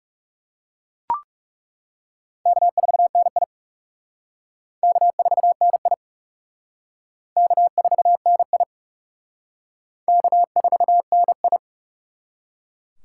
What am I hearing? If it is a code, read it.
Morse code, heard as a keyed tone, about 36 wpm, 700 Hz. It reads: K4NI K4NI K4NI K4NI
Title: Speed Racing - USA 1x2 Call Signs 30wpm